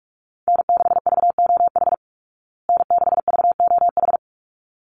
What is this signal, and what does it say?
Morse code, heard as a keyed tone, about 44 wpm, 700 Hz. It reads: N6VOH N6VOH
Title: Call Signs Repeated 3x - 45wpm